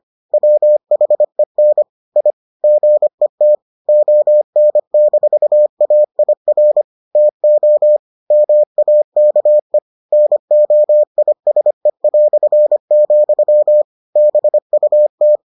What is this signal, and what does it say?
Morse code, heard as a keyed tone, about 25 wpm, 600 Hz. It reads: WHEN I GET ON-AIR TO MAKE NOISE", BUT